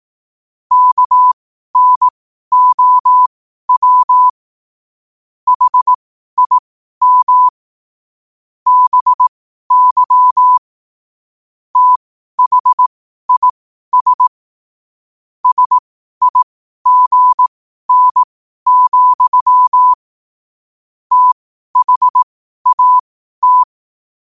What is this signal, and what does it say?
Morse code, heard as a keyed tone, about 18 wpm, spaced as for 12 wpm, 1000 Hz. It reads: KNOW HIM BY THIS SIGN, THAT
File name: qotdfiles / 12wpm